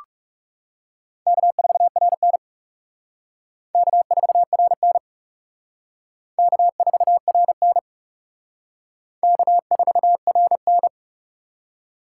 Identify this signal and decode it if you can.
Morse code, heard as a keyed tone, about 36 wpm, 700 Hz. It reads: K4RN K4RN K4RN K4RN
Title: Speed Racing - USA Call Signs 30wpm